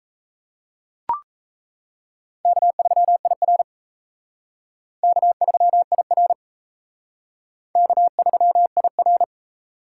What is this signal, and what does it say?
Morse code, heard as a keyed tone, about 35 wpm, 700 Hz. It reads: K3IR K3IR K3IR
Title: Speed Racing - USA 1x2 Call Signs 28wpm